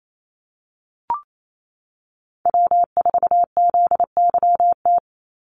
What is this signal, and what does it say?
Morse code, heard as a keyed tone, about 28 wpm, 700 Hz. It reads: W4ZYT
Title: Call Signs - 28wpm